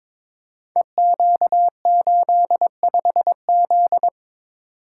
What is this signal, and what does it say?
Morse code, heard as a keyed tone, about 22 wpm, 700 Hz. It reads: EQ85Z
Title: Code Groups - 5 Letter:Numbers - 22wpm